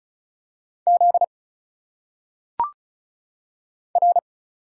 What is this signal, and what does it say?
Morse code, heard as a keyed tone, about 36 wpm, 700 Hz. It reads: Z R
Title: Single Letters - Phonetic Pronunciation 35wpm